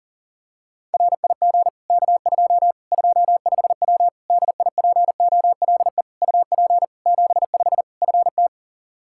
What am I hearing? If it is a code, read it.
Morse code, heard as a keyed tone, about 37 wpm, 700 Hz. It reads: RIG K2 25W DIPOLE UP 75 FT
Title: Practice QSO - Exchange 2 - Info - 40wpm